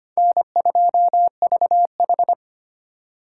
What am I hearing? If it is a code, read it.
Morse code, heard as a keyed tone, about 25 wpm, 700 Hz. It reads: N2VH
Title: USA Call Signs - 4 Characters - 4x Character Spacing - 25wpm